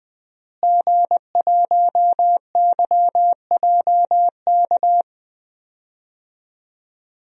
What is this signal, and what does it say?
Morse code, heard as a keyed tone, about 20 wpm, 700 Hz. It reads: G1YJK
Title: DXCC-100 Call Signs - 5 Characters - Repeated 2x - 20wpm